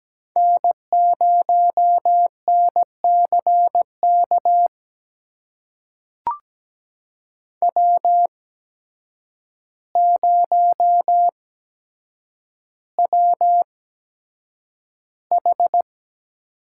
Morse code, heard as a keyed tone, about 17 wpm, spaced as for 8 wpm, 700 Hz. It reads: N0NCK W0WH
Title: USA Call Signs - 5 Characters - 8x Character Spacing - 17wpm